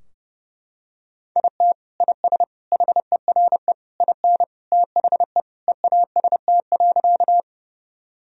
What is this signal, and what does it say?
Morse code, heard as a keyed tone, about 32 wpm, 700 Hz. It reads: IT IS HERE IN THE EAST.
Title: Sentences from Top 1000 Words - Review Part 3 - 30wpm